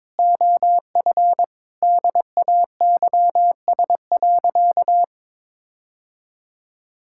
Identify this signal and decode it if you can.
Morse code, heard as a keyed tone, about 22 wpm, 700 Hz. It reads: OF DAYS.
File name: practice / 22wpm / Sentences with 4-Word Phrases - 22wpm